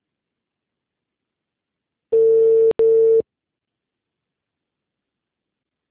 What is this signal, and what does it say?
Ringback tone (China, 450 Hz).
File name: out- -2002